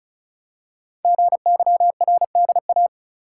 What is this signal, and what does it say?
Morse code, heard as a keyed tone, about 36 wpm, 700 Hz. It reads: GYRDA
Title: Code Groups - 5 Letters - 35wpm